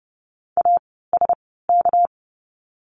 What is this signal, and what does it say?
Morse code, heard as a keyed tone, about 30 wpm, spaced as for 17 wpm, 700 Hz. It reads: ASK
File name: course / 30:10wpm / 230 Lesson 37f - Letter x - Words 30wpm